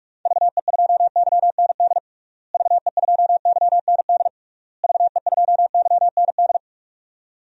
Morse code, heard as a keyed tone, about 46 wpm, 700 Hz. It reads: VE2YND VE2YND VE2YND
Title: DXCC-100 Call Signs - 6 Characters - Repeated 3x - 45wpm